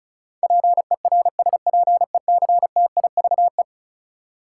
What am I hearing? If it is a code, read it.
Morse code, heard as a keyed tone, about 36 wpm, 700 Hz. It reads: PERSPECTIVE
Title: Sentences from Top 1200 Words - New Word Drill - 35wpm